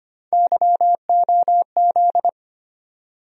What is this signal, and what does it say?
Morse code, heard as a keyed tone, about 25 wpm, 700 Hz. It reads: YOZ
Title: USA Call Sign Postfixes - 3 Letters 25wpm